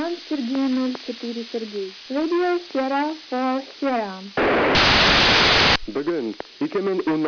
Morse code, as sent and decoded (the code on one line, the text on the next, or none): none